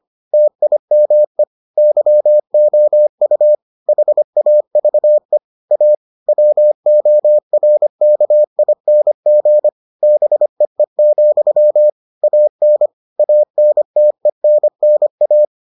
Morse code, heard as a keyed tone, about 25 wpm, 600 Hz. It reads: TIME YOU HAVE A WORKING BEE, AN ANTENNA